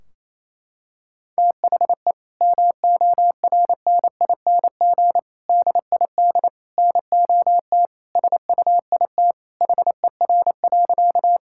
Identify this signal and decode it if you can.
Morse code, heard as a keyed tone, about 28 wpm, 700 Hz. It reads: THE MORNING DID NOT SUIT HER.
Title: Sentences from Top 1600 Words 28wpm